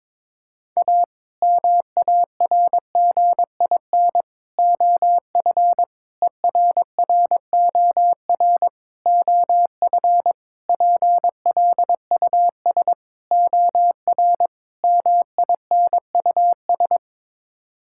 Morse code, heard as a keyed tone, about 22 wpm, 700 Hz. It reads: A MARGIN OF ERROR OF PLUS OR MINUS
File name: practice / 22wpm / Sets of 8 Words - 22wpm